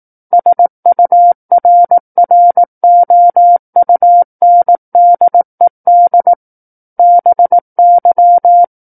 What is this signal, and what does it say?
Morse code, heard as a keyed tone, about 18 wpm, 700 Hz. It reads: SURROUNDED BY
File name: 18 WPM